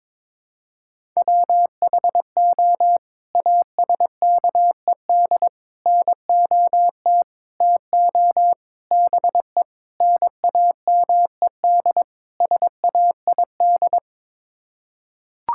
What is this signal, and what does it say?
Morse code, heard as a keyed tone, about 22 wpm, 700 Hz. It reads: WHO ASKED NOT TO BE NAMED SAID
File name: practice / 22wpm / Sets of 7 Words - 22wpm